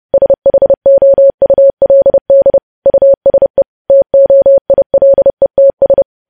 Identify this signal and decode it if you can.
Morse code, heard as a keyed tone, about 30 wpm, 550 Hz. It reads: SHOULD USE TOILETS
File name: podcast / News headlines-1x-30x30